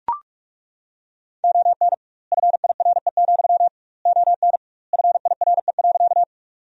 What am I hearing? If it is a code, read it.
Morse code, heard as a keyed tone, about 43 wpm, 700 Hz. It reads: ON FIRE, ON FIRE.
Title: 2x Word-Spacing - Sentences from Top 400 Words - 45wpm